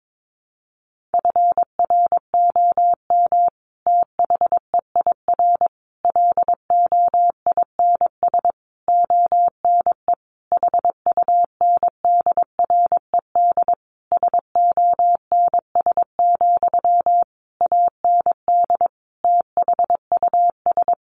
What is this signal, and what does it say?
Morse code, heard as a keyed tone, about 22 wpm, 700 Hz. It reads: FROM THEIR LOINS ONE HUNDRED SONS, AND THUS
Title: A Sound Of Thunder 22wpm